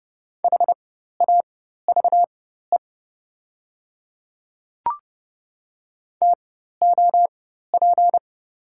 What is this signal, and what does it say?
Morse code, heard as a keyed tone, about 30 wpm, spaced as for 14 wpm, 700 Hz. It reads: HAVE TOP
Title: 4x Character Spacing - Top 500 Words - 30wpm